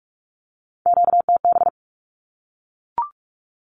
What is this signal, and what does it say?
Morse code, heard as a keyed tone, about 44 wpm, 700 Hz. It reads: QTB
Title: Q-Codes Repeated 3x 45wpm